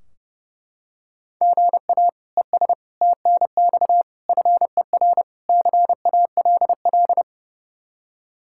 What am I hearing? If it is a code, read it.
Morse code, heard as a keyed tone, about 30 wpm, 700 Hz. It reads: GA ES TNX FER CALL
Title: Practice QSO - Exchange 1 - Beginning - 30wpm